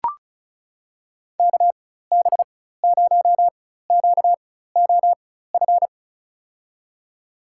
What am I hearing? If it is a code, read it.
Morse code, heard as a keyed tone, about 36 wpm, spaced as for 16 wpm, 700 Hz. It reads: KB0QOF
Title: USA Call Signs - 6 Characters - 4x Character Spacing - 35wpm